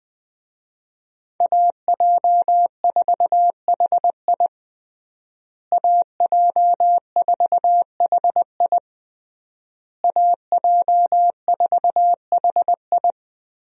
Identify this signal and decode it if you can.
Morse code, heard as a keyed tone, about 20 wpm, 700 Hz. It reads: AJ4HI AJ4HI AJ4HI
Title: USA Call Signs - 5 Characters - Repeated 3x - 20wpm